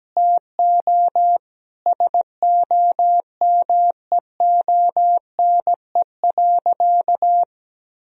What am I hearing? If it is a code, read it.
Morse code, heard as a keyed tone, about 17 wpm, 700 Hz. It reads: TO SOMEONE.
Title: Sentences from Top 1000 Words - Review Part 4 - 17wpm